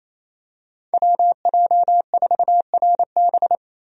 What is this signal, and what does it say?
Morse code, heard as a keyed tone, about 28 wpm, 700 Hz. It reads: WJ4RB
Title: USA Call Signs - 5 Characters - Repeated 3x - 28wpm